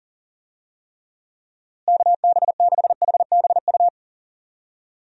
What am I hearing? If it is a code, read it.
Morse code, heard as a keyed tone, about 40 wpm, 700 Hz. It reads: KB6HBU
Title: USA Call Signs - 6 Characters - Repeated 2x - 40wpm